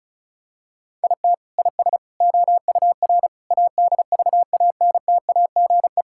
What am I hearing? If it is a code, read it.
Morse code, heard as a keyed tone, about 36 wpm, 700 Hz. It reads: IT IS OUR ADVANTAGE
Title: Sentences from Top 1000 Words - Review Part 6 - 35wpm